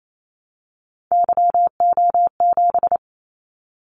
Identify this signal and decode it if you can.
Morse code, heard as a keyed tone, about 28 wpm, 700 Hz. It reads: YO7
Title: Callsign Prefixes for Top 25 DXCC - 28wpm